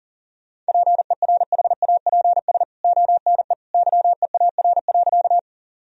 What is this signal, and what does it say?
Morse code, heard as a keyed tone, about 40 wpm, 700 Hz. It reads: PERHAPS ONE YEAR.
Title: Sentences from Top 1000 Words - Review Part 1 - 40wpm